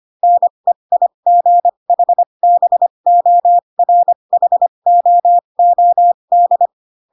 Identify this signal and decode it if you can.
Morse code, heard as a keyed tone, about 25 wpm, spaced as for 22 wpm, 700 Hz. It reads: NEIGHBORHOOD
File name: Pinocchio / Pinocchio ch2 18 25 ik1hge